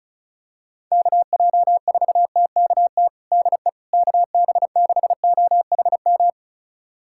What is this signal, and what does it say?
Morse code, heard as a keyed tone, about 36 wpm, 700 Hz. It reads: KJ4TKT DE KB6OHM
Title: USA Callsign DE Callsign 35wpm